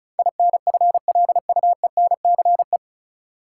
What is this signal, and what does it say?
Morse code, heard as a keyed tone, about 36 wpm, 700 Hz. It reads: INFLUENCE
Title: Sentences from Top 1100 Words - New Word Drill - 35wpm